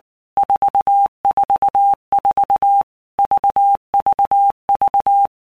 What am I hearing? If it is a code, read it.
Morse code, heard as a keyed tone, about 19 wpm, 800 Hz. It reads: 444 VVV
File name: Basic Letter Groups By Blocks / Groups@18 / VB46@18